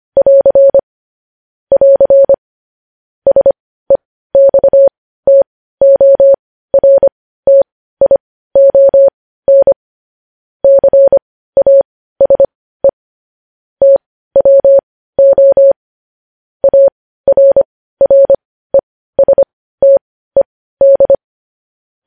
Morse code, heard as a keyed tone, about 25 wpm, spaced as for 15 wpm, 550 Hz. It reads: <AR> <AR> SEXTORTION CASE TWO ARRESTED
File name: podcast / News headlines-1x-25x15